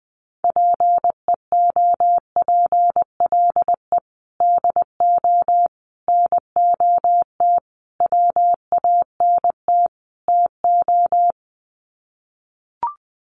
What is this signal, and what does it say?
Morse code, heard as a keyed tone, about 19 wpm, 700 Hz. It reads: PEOPLE DO NOT WANT TO
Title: Sets of 5 Words Encore2 20wpm